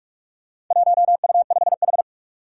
Morse code, heard as a keyed tone, about 46 wpm, 700 Hz. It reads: 1U5H
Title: Four Challenging Letter-Number Combinations 45wpm